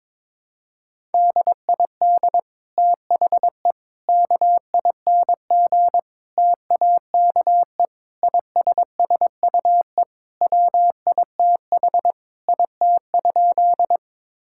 Morse code, heard as a keyed tone, about 22 wpm, 700 Hz. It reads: DID THE KING TAKE ISSUE WITH IT?
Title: Sentences from Top 1000 Words - Review Part 6 - 22wpm